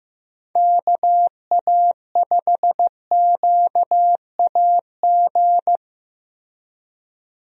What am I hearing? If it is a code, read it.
Morse code, heard as a keyed tone, about 15 wpm, 700 Hz. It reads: KA5QAG
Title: USA Call Signs - 6 Characters - Repeated 3x - 15wpm